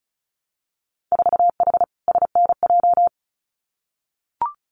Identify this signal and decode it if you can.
Morse code, heard as a keyed tone, about 35 wpm, 700 Hz. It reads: 4H SNJ
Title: Winter Field Day Exchanges Encore 35wpm